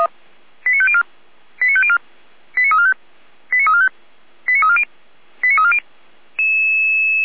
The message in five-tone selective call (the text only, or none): 89504 89504 89345 89345 89340 89340